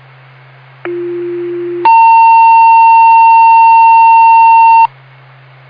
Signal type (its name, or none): two-tone page